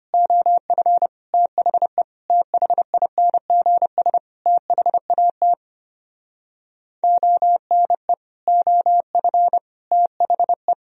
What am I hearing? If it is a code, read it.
Morse code, heard as a keyed tone, about 33 wpm, 700 Hz. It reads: OF THE THINGS THAT ONE OF THE